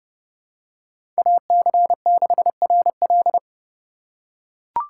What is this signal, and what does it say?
Morse code, heard as a keyed tone, about 30 wpm, 700 Hz. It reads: AC6RL
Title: USA Call Signs - 5 Characters - 3x Character Spacing - 30wpm